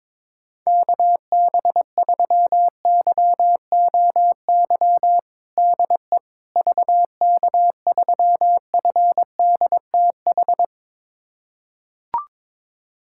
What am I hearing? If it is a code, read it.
Morse code, heard as a keyed tone, about 22 wpm, 700 Hz. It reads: KB3YOY DE VK3FDTH
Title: USA Callsign DE Callsign 22wpm